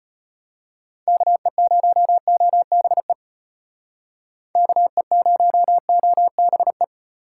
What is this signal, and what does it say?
Morse code, heard as a keyed tone, about 34 wpm, 700 Hz. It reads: KE0OBE KE0OBE